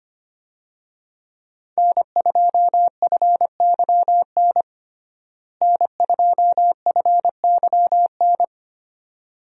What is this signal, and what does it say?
Morse code, heard as a keyed tone, about 25 wpm, 700 Hz. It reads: N2FYN N2FYN
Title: USA Call Signs - 5 Characters - Repeated 2x - 25wpm